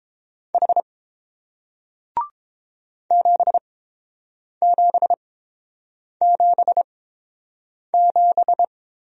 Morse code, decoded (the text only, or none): H 7777